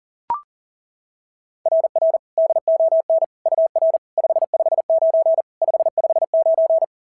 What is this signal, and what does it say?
Morse code, heard as a keyed tone, about 40 wpm, 660 Hz. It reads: RR DON UR 559 559
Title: SOTA QSO Practice 40wpm